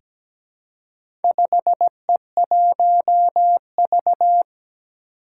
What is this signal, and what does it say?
Morse code, heard as a keyed tone, about 17 wpm, 700 Hz. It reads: 5E1V
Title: Four Challenging Letter-Number Combinations 17wpm